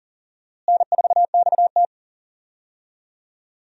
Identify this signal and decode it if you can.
Morse code, heard as a keyed tone, about 40 wpm, 700 Hz. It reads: N4XT
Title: USA Call Signs - 4 Characters - Repeated 2x - 40wpm